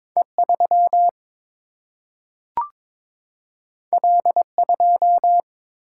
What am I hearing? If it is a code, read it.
Morse code, heard as a keyed tone, about 22 wpm, 700 Hz. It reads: E 3 L 2
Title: Single-Letter Single-Number 22wpm